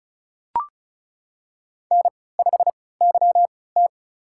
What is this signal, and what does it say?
Morse code, heard as a keyed tone, about 34 wpm, spaced as for 20 wpm, 700 Hz. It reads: N5YT
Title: USA Call Signs - 4 Characters - 3x Character Spacing - 35wpm